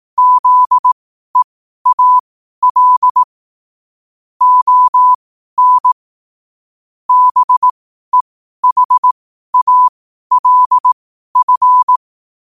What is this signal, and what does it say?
Morse code, heard as a keyed tone, about 18 wpm, spaced as for 12 wpm, 1000 Hz. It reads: ZEAL ON BEHALF
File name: qotdfiles / 12wpm